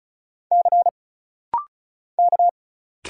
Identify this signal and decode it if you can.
Morse code, heard as a keyed tone, about 35 wpm, 700 Hz. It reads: C K
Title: Single Letter-Number - Rapid-Fire 35wpm